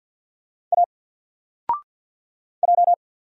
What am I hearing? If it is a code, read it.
Morse code, heard as a keyed tone, about 48 wpm, 700 Hz. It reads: A J
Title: Single Letter-Number 50wpm